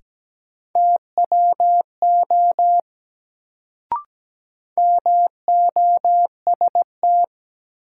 Morse code, heard as a keyed tone, about 17 wpm, 700 Hz. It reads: TWO MOST
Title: Sets of 2 Words - New Word Drill - 17wpm